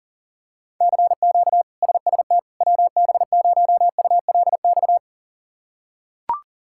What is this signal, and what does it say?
Morse code, heard as a keyed tone, about 40 wpm, 700 Hz. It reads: CQ SST WB0ULX
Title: K1USN SST QSO Practice - 40wpm